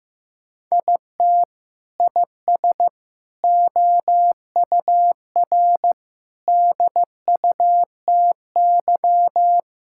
Morse code, heard as a keyed tone, about 15 wpm, 700 Hz. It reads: IT IS OUR DUTY
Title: Sentences from Top 1600 Words 15wpm